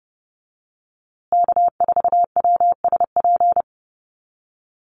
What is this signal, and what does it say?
Morse code, heard as a keyed tone, about 30 wpm, 700 Hz. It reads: K4WSP
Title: USA Call Signs Repeated 2x 30wpm